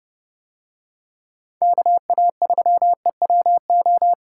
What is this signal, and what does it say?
Morse code, heard as a keyed tone, about 30 wpm, 700 Hz. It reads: KA3EWO